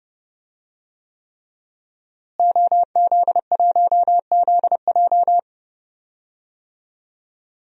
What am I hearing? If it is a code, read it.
Morse code, heard as a keyed tone, about 32 wpm, 700 Hz. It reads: OZ1ZJ
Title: DXCC-100 Call Signs - 5 Characters - Repeated 3x - 30wpm